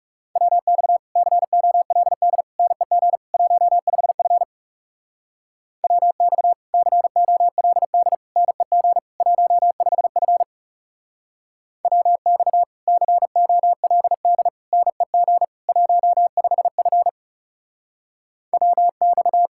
Morse code, heard as a keyed tone, about 36 wpm, 700 Hz. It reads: WX COLD NEG 15F WX COLD NEG 15F WX COLD NEG 15F WX
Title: Speed-Racing Common QSO Phrases - 30wpm